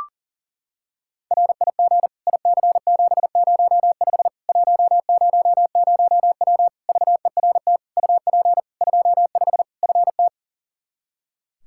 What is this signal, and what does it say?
Morse code, heard as a keyed tone, about 40 wpm, 700 Hz. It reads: RIG IC705 100W VERT UP 25 FT
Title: K1USN SST QSO Practice 40wpm